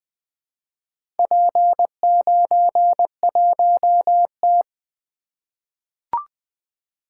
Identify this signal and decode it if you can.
Morse code, heard as a keyed tone, about 21 wpm, 700 Hz. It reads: P91T
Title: Four Letter-Number Combinations - 20wpm